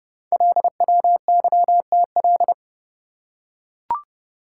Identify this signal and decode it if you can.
Morse code, heard as a keyed tone, about 29 wpm, 690 Hz. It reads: LWYTL
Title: Code Groups - 5 Letters - 30wpm